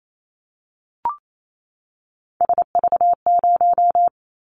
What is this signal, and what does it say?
Morse code, heard as a keyed tone, about 28 wpm, 700 Hz. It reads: SV0
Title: Callsign Prefixes for Top 50 DXCC - 28wpm